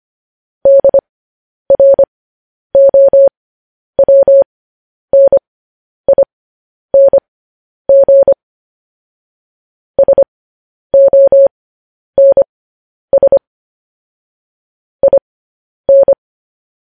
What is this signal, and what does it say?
Morse code, heard as a keyed tone, about 25 wpm, spaced as for 10 wpm, 550 Hz. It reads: DROWNING SONS IN
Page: news headlines in morse code at 10 wpm